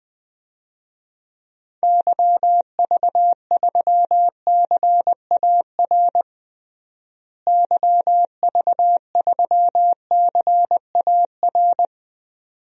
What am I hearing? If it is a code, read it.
Morse code, heard as a keyed tone, about 20 wpm, 700 Hz. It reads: YV3CAR YV3CAR
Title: DXCC-100 Call Signs - 6 Characters - Repeated 2x - 20wpm